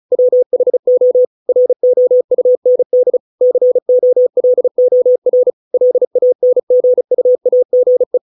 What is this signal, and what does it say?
Morse code, heard as a keyed tone, about 34 wpm, 500 Hz. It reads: WHO ROUND COLOR LANGUAGE